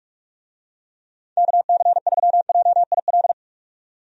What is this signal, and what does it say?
Morse code, heard as a keyed tone, about 46 wpm, 700 Hz. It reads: KC3JIL
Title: USA Call Signs - 6 Characters - 45wpm